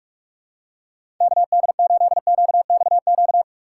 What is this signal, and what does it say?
Morse code, heard as a keyed tone, about 41 wpm, 700 Hz. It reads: KD8QXQ
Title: USA Call Signs - 6x Character Spacing - 45wpm